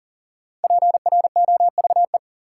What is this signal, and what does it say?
Morse code, heard as a keyed tone, about 39 wpm, 700 Hz. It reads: PROVE
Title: Sentences from Top 1600 Words - New Word Drill - 40wpm